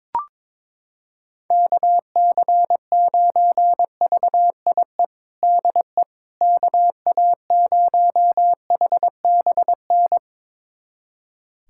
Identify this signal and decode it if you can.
Morse code, heard as a keyed tone, about 22 wpm, 700 Hz. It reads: KC9VIE DE KA0HBN
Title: USA Callsign DE Callsign - 6 Characters - 22wpm